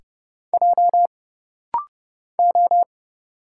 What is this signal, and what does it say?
Morse code, heard as a keyed tone, about 29 wpm, 700 Hz. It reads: J O
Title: Single Letters - Rapid-Fire 30wpm